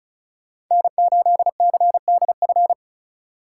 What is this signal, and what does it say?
Morse code, heard as a keyed tone, about 37 wpm, 700 Hz. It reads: N8CDF
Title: Speed Racing - USA 1x3 Call Signs 30wpm